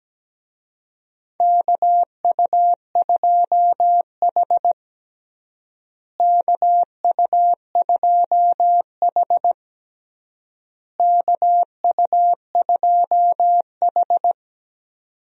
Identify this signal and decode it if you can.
Morse code, heard as a keyed tone, about 17 wpm, 700 Hz. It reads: KU2H KU2H KU2H
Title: USA Call Signs - 4 Characters - Repeated 3x - 17wpm